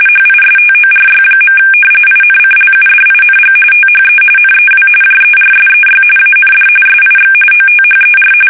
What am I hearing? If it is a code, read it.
RTTY, about 45.45 baud. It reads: HISTORY. INCLUDING THE NEW JERSEY AND SIX OTHER
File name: fsk850 / battleshipnj